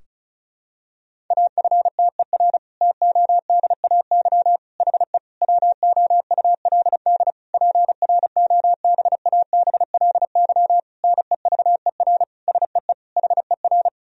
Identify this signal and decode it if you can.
Morse code, heard as a keyed tone, about 34 wpm, 700 Hz. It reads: AFTER TODAY HE WOULD PROBABLY NEVER SEE HER
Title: Sentences from Top 1000 Words - Review Part 4 - 35wpm